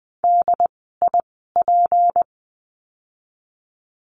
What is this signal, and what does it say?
Morse code, heard as a keyed tone, about 20 wpm, spaced as for 15 wpm, 700 Hz. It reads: DIP
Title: 153 Lesson 25f - Letter p - Words 20wpm